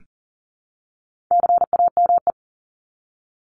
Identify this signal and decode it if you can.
Morse code, heard as a keyed tone, about 40 wpm, 700 Hz. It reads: CAME
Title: Top 200 Words Repeated 40wpm